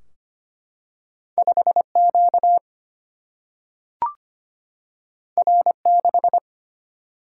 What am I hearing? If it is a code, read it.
Morse code, heard as a keyed tone, about 25 wpm, 700 Hz. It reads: 5Q R6
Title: Single-Letter Single-Number 25wpm